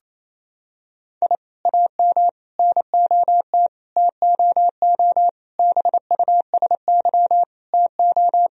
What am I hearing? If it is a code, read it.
Morse code, heard as a keyed tone, about 28 wpm, 700 Hz. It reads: I AM NOT TOO BUSY TO
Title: Sentences from Top 1000 Words - Review Part 3 - 28wpm